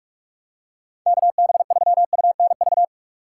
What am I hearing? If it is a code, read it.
Morse code, heard as a keyed tone, about 45 wpm, 700 Hz. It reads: KB3UNV
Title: USA Call Signs - 6 Characters - 8x Character Spacing - 45wpm